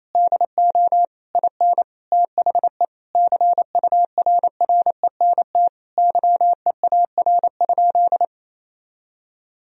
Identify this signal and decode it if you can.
Morse code, heard as a keyed tone, about 28 wpm, 700 Hz. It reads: DO IN THE CURRENT YEAR?
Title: Sentences from Top 1000 Words - Review Part 4 - 28wpm